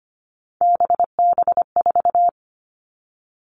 Morse code, heard as a keyed tone, about 25 wpm, 700 Hz. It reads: BB4
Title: All Cut Numbers - 3-Digit 25wpm